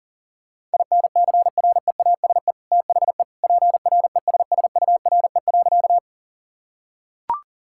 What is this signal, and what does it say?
Morse code, heard as a keyed tone, about 37 wpm, 700 Hz. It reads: INCREASE THE PRESSURE.
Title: Sentences from Top 1000 Words - Review Part 6 - 40wpm